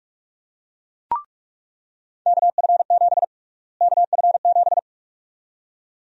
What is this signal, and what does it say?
Morse code, heard as a keyed tone, about 46 wpm, 700 Hz. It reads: KF7 KF7
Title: USA Call Sign Prefixes - Repeated 2x - 45wpm